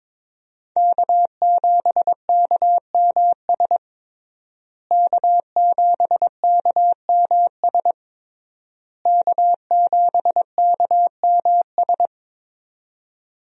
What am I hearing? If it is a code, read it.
Morse code, heard as a keyed tone, about 22 wpm, 700 Hz. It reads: K7KMS K7KMS K7KMS
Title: USA Call Signs - 5 Characters - Repeated 3x - 22wpm